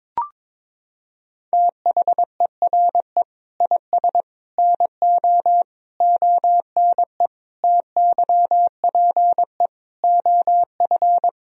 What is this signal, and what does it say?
Morse code, heard as a keyed tone, about 22 wpm, 700 Hz. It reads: THERE IS NO ONE TYPE OF
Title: Sentences from Top 1000 Words - Review Part 3 - 22wpm